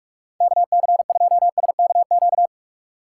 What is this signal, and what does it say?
Morse code, heard as a keyed tone, about 46 wpm, 700 Hz. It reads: KC2SKQ
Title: USA Call Signs - 6 Characters - 2x Character Spacing - 45wpm